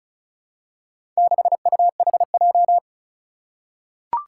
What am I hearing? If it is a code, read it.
Morse code, heard as a keyed tone, about 36 wpm, 700 Hz. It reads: 6UHJ
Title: Four Challenging Letter-Number Combinations 35wpm